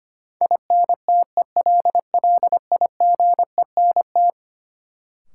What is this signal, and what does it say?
Morse code, heard as a keyed tone, about 25 wpm, 700 Hz. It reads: INTELLIGENT
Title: Sentences from Top 1000 Words - Review Part 7 - New Word Drill - 25wpm